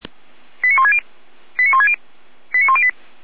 Five-tone selective call (ZVEI), heard as quick runs of five tones, tones of about 70 ms. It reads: 89160 89160 89108